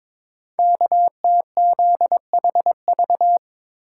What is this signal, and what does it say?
Morse code, heard as a keyed tone, about 22 wpm, 700 Hz. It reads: KTZHV
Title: Code Groups - 5 Letters - 22wpm